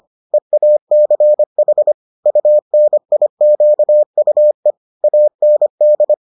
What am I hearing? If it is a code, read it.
Morse code, heard as a keyed tone, about 25 wpm, 600 Hz. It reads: EACH UNIQUE AND